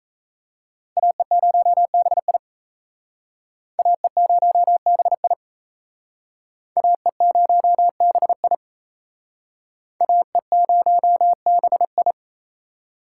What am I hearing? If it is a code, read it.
Morse code, heard as a keyed tone, about 35 wpm, 700 Hz. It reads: AE0BI AE0BI AE0BI AE0BI
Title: Speed Racing - USA 2x2 Call Signs 28wpm